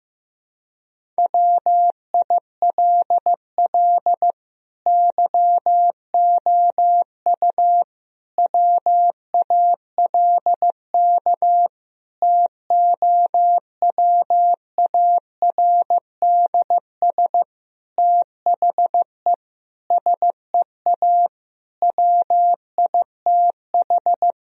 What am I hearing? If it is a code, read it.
Morse code, heard as a keyed tone, about 16 wpm, 700 Hz. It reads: WILL YOU WALK TOWARDS THE SEA WITH
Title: Sentences from Top 1000 Words - Review Part 2 - 15wpm